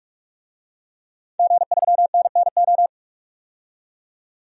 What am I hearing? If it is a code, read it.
Morse code, heard as a keyed tone, about 41 wpm, 700 Hz. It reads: G3NNO
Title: DXCC-100 Call Signs - 5 Characters - 45wpm